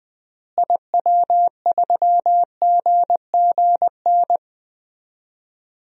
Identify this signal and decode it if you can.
Morse code, heard as a keyed tone, about 20 wpm, 700 Hz. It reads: IW3GGN
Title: DXCC-100 Call Signs - 6 Characters - Repeated 2x - 20wpm